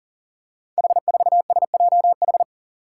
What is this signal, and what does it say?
Morse code, heard as a keyed tone, about 40 wpm, 700 Hz. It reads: H4SJH
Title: Five Challenging Letter-Number Combinations 40wpm